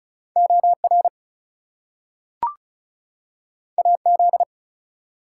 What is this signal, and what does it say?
Morse code, heard as a keyed tone, about 36 wpm, 700 Hz. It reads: OR AZ
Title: US States and Canadian Provinces 35wpm